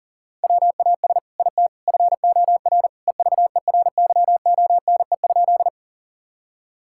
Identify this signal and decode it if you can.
Morse code, heard as a keyed tone, about 40 wpm, 700 Hz. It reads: WAS IT FOR EVERYONE?
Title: Sentences from Top 1000 Words - Review Part 5 - 40wpm